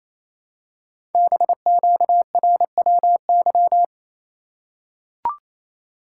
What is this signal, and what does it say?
Morse code, heard as a keyed tone, about 29 wpm, 700 Hz. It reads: BQRWY
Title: Code Groups - 5 Letters - 28wpm